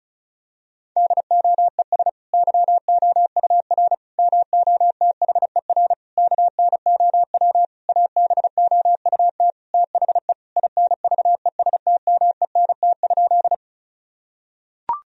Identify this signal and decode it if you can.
Morse code, heard as a keyed tone, about 34 wpm, 700 Hz. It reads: DOES YOUR MOTHER KNOW ABOUT THE INVESTMENT?
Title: Sentences from Top 1000 Words - Review Part 4 - 35wpm